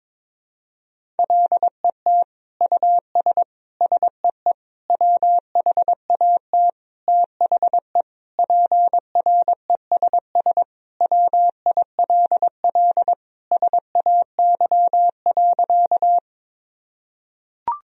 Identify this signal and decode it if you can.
Morse code, heard as a keyed tone, about 22 wpm, 700 Hz. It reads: LET US SEE WHAT THE PRESS WILL SAY.
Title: Sentences from Top 1000 Words - Review Part 1 - 22wpm